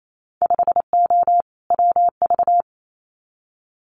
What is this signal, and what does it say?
Morse code, heard as a keyed tone, about 28 wpm, 700 Hz. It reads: 5O WV
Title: Winter Field Day Exchanges Encore 28wpm